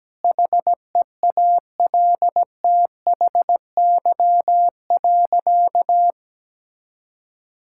Morse code, heard as a keyed tone, about 17 wpm, 700 Hz. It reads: HEALTHY.
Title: Sentences from Top 1000 Words - Review Part 5 - 17wpm